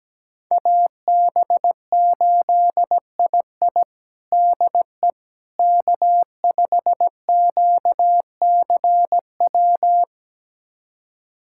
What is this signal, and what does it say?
Morse code, heard as a keyed tone, about 17 wpm, 700 Hz. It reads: AB8II DE K5QCW
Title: USA Callsign DE Callsign - 5 Characters - 17wpm